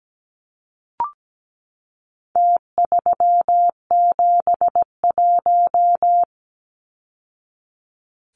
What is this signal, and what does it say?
Morse code, heard as a keyed tone, about 17 wpm, 700 Hz. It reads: T371
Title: Numbers - Contest Style - 4-Digit 17wpm